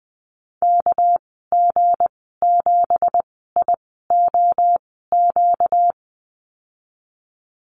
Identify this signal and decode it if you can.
Morse code, heard as a keyed tone, about 20 wpm, spaced as for 15 wpm, 700 Hz. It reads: KG7IOQ
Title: 238 Lesson 38g - Letter q - Call Signs 20wpm